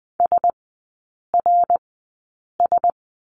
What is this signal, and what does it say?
Morse code, heard as a keyed tone, about 20 wpm, spaced as for 8 wpm, 700 Hz. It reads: SRS
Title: 053 Lesson 10e - Letter r - Three Characters 20wpm